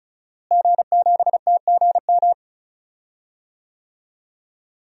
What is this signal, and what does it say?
Morse code, heard as a keyed tone, about 36 wpm, 700 Hz. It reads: G7TGM
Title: DXCC-100 Call Signs - 5 Characters - Repeated 2x - 35wpm